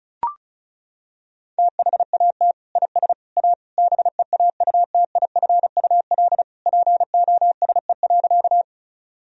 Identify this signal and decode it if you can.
Morse code, heard as a keyed tone, about 36 wpm, 700 Hz. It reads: THAT IS A BEAUTIFUL POSE.
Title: Sentences from Top 1000 Words - Review Part 2 - 35wpm